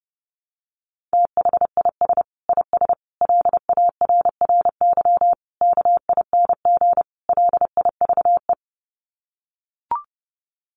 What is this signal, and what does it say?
Morse code, heard as a keyed tone, about 28 wpm, 700 Hz. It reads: THIS IS LARRY KING LIVE
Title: Sets of 5 Words Encore2 30wpm